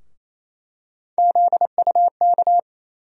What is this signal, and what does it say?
Morse code, heard as a keyed tone, about 28 wpm, 700 Hz. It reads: ZUK
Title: USA Call Sign Postfixes - 3 Letters 28wpm